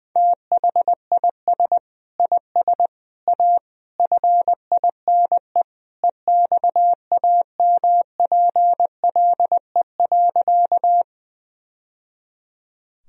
Morse code, heard as a keyed tone, about 20 wpm, 700 Hz. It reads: THIS IS A FINE EXAMPLE.
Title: Sentences from Top 1000 Words - Review Part 1 - 20wpm